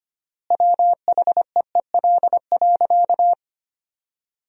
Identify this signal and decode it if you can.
Morse code, heard as a keyed tone, about 25 wpm, 700 Hz. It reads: WHEEL.
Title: 3x Word-Spacing - Sentences from Top 500 Words - 25wpm